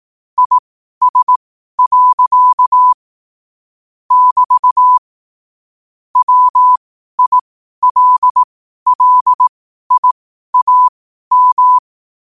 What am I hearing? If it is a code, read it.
Morse code, heard as a keyed tone, about 18 wpm, spaced as for 12 wpm, 1000 Hz. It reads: IS. <BT> WILLIAM
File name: qotdfiles / 12wpm